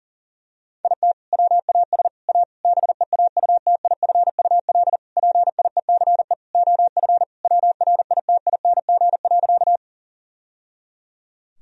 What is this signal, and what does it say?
Morse code, heard as a keyed tone, about 40 wpm, 700 Hz. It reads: IT WAS A BEAUTIFUL PIECE OF WRITING.
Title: Sentences from Top 1000 Words - Review Part 2 - 40wpm